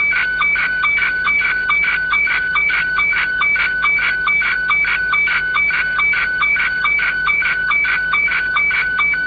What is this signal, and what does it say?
SSTV (slow-scan television), Scottie 1.